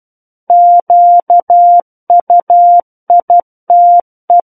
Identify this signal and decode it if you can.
Morse code, heard as a keyed tone, about 12 wpm, 700 Hz. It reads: QUITE
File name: Book One / 03-Chapter 3 - 12 WPM